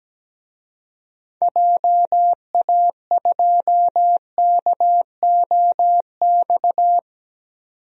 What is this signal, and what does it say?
Morse code, heard as a keyed tone, about 17 wpm, 700 Hz. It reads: JA2KOX
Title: DXCC-100 Call Signs - 6 Characters - 17wpm